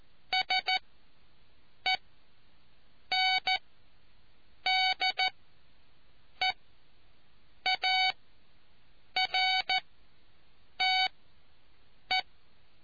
Morse code, heard as a keyed tone, about 14 wpm, spaced as for 6 wpm, 750 Hz. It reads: SENDEARTE